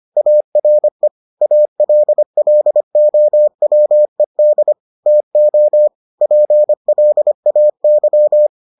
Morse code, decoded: ARE ALLOWED TO PLAY

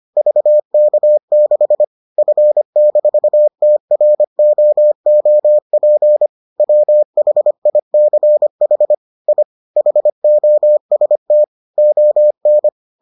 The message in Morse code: VK6 F-TROOP WHICH I HOST ON

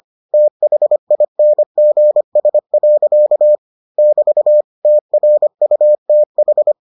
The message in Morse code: THINGS. <BT> TRUTH

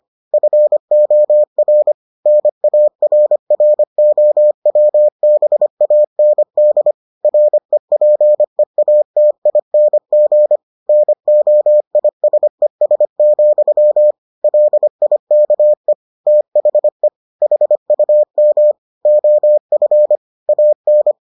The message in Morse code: FOR NARROWBAND REPEATING NOISES, LIKE THE HUM OF AN

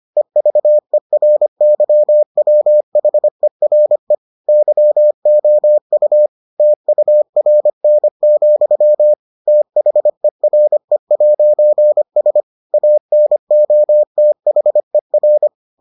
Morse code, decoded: EVERYWHERE YOU TURN, THERE'S ANOTHER

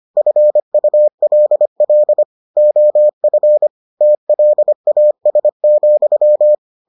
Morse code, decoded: FULL OF TLAS,

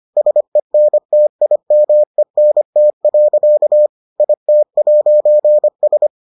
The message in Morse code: SENTIMENT. IT'S

